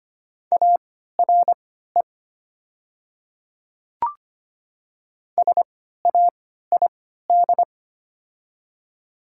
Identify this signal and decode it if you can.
Morse code, heard as a keyed tone, about 25 wpm, spaced as for 14 wpm, 700 Hz. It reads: ARE SAID
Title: 3x Character Spacing - Top 200 Words - 25wpm